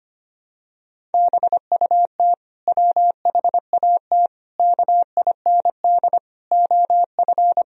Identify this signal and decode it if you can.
Morse code, heard as a keyed tone, about 25 wpm, 700 Hz. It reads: BUT WHAT KIND OF
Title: Sentences from Top 1000 Words - Review Part 1 - 25wpm